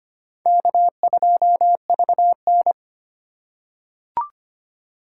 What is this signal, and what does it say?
Morse code, heard as a keyed tone, about 25 wpm, 700 Hz. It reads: K2VN
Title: USA Call Signs - 4 Characters - 2x Character Spacing - 25wpm